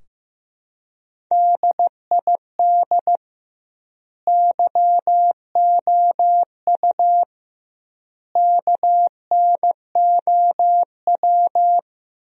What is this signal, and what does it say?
Morse code, heard as a keyed tone, about 15 wpm, 700 Hz. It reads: DID YOU KNOW
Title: 2x Word-Spacing - Sentences from Top 100 Words - 15wpm